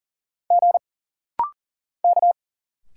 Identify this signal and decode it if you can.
Morse code, heard as a keyed tone, about 40 wpm, 700 Hz. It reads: G K
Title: Single Letters - Rapid-Fire 40wpm